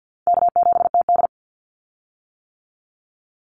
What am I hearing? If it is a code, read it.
Morse code, heard as a keyed tone, about 50 wpm, 700 Hz. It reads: K7TD
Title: Easy Call Signs - 50wpm